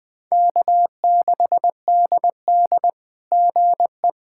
Morse code, heard as a keyed tone, about 20 wpm, 700 Hz. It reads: K6DD GE
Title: SOTA QSO Practice 20wpm